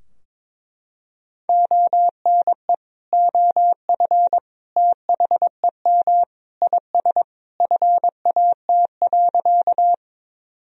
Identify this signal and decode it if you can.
Morse code, heard as a keyed tone, about 21 wpm, 700 Hz. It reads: ONE OF THEM IS FAT.
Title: Sentences from Top 1000 Words - Review Part 4 - 22wpm